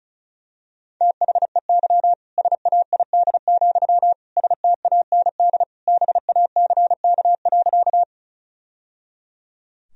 Morse code, heard as a keyed tone, about 34 wpm, 700 Hz. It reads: THEY SAID, STAND BACK.
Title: Sentences from Top 1000 Words - Review Part 1 - 35wpm